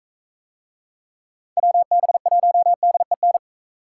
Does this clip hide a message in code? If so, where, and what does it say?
Yes: Morse code from 1.6 to 3.4 s, heard as a keyed tone, about 43 wpm, 700 Hz. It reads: WB1DEN